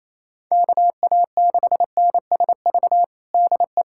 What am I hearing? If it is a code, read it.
Morse code, heard as a keyed tone, about 28 wpm, 700 Hz. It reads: KA6NSV DE
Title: USA Callsign DE Callsign 28wpm